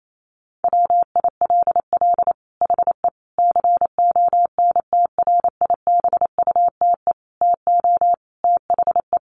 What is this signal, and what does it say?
Morse code, heard as a keyed tone, about 28 wpm, 700 Hz. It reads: WILL HE CONTRIBUTE TO THE
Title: Sentences from Top 1400 Words 28wpm